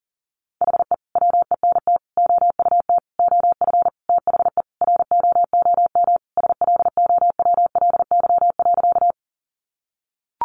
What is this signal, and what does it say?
Morse code, heard as a keyed tone, about 40 wpm, 700 Hz. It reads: HE WENT OUT OF THE ROOM SLOWLY.
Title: Sentences from Top 1100 Words 40wpm